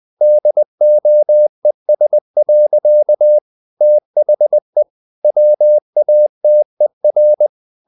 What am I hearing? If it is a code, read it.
Morse code, heard as a keyed tone, about 20 wpm, 600 Hz. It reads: DOES. THE WATER
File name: the old man and the sea part 2 20wpm